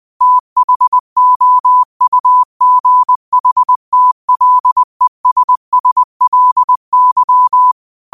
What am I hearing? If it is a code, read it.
Morse code, heard as a keyed tone, about 20 wpm, 1000 Hz. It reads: THOUGHTLESSLY
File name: qotdfiles / 20wpm